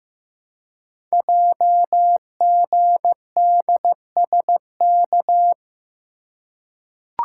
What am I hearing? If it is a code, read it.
Morse code, heard as a keyed tone, about 15 wpm, 700 Hz. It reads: JGDSK
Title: Code Groups - 5 Letter:Numbers - 15wpm